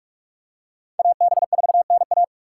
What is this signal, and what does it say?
Morse code, heard as a keyed tone, about 44 wpm, 700 Hz. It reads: AB4NA